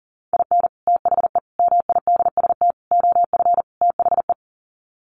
Morse code, heard as a keyed tone, about 40 wpm, 700 Hz. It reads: IN THE MIDST OF THE
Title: Sets of 5 Words 40wpm